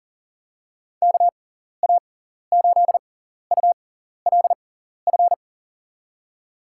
Morse code, heard as a keyed tone, about 38 wpm, spaced as for 14 wpm, 700 Hz. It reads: KA8ULF
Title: USA Call Signs - 6 Characters - 6x Character Spacing - 40wpm